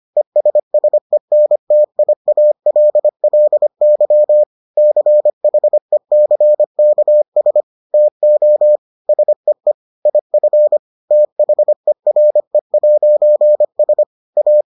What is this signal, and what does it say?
Morse code, heard as a keyed tone, about 25 wpm, 600 Hz. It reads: ESSENTIALLY CHECKS TO SEE IF THERE'S A